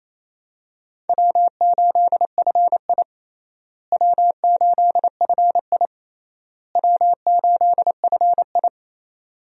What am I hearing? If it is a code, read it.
Morse code, heard as a keyed tone, about 27 wpm, 700 Hz. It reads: W8FI W8FI W8FI
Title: USA Call Signs - 4 Characters - Repeated 3x - 28wpm